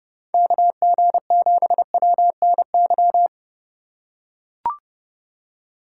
Morse code, heard as a keyed tone, about 32 wpm, 700 Hz. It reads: KG7WNY
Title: USA Call Signs - 6 Characters - 3x Character Spacing - 30wpm